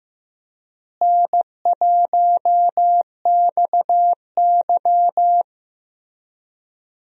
Morse code, heard as a keyed tone, about 15 wpm, 700 Hz. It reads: N1XY
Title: USA Call Signs - 4 Characters - Repeated 3x - 15wpm